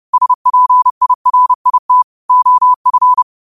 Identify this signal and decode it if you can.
Morse code, heard as a keyed tone, about 30 wpm, 1000 Hz. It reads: SPIRIT OF
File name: qotdfiles / 30wpm